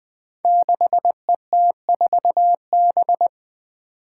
Morse code, heard as a keyed tone, about 21 wpm, 700 Hz. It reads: 6ET4B
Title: Five Challenging Letter-Number Combinations 20wpm